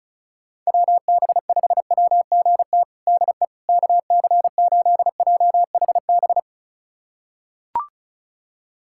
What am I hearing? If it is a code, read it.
Morse code, heard as a keyed tone, about 34 wpm, 700 Hz. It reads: WB5WGT DE KC8JHB
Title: USA Callsign DE Callsign 35wpm